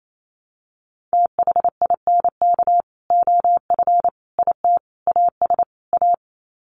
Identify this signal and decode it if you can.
Morse code, heard as a keyed tone, about 28 wpm, 700 Hz. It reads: THINK OF IT AS A
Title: Sets of 5 Words Encore2 28wpm